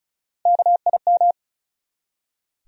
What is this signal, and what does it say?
Morse code, heard as a keyed tone, about 36 wpm, 700 Hz. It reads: KIM
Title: USA Common Names - Max 4 Letters - 35wpm